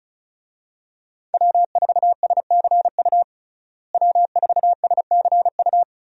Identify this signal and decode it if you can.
Morse code, heard as a keyed tone, about 36 wpm, 700 Hz. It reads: W4SCU W4SCU
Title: USA Call Signs - 5 Characters - Repeated 2x - 35wpm